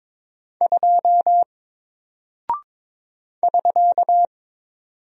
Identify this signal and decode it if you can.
Morse code, heard as a keyed tone, about 22 wpm, 700 Hz. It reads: 2 <SK>
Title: Single Character (Letter:Number:Punctuation:Prosign) Alt 22wpm